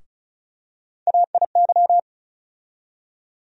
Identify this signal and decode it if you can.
Morse code, heard as a keyed tone, about 35 wpm, 700 Hz. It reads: AIY
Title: USA Call Sign Postfixes - 3 Letters 35wpm